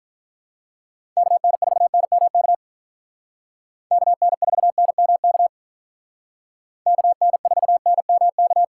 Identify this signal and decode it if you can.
Morse code, heard as a keyed tone, about 43 wpm, 700 Hz. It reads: KN4NMK KN4NMK KN4NMK
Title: Speed Racing - USA 2x3 Call Signs 35wpm